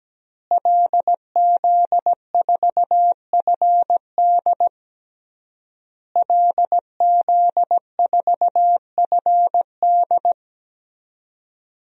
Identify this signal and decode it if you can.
Morse code, heard as a keyed tone, about 17 wpm, 700 Hz. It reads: LZ4FD LZ4FD
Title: DXCC-100 Call Signs - 5 Characters - Repeated 3x - 17wpm